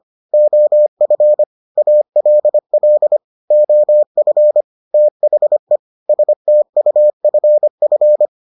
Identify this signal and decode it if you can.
Morse code, heard as a keyed tone, about 25 wpm, 600 Hz. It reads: OF ALL OF THE STUFF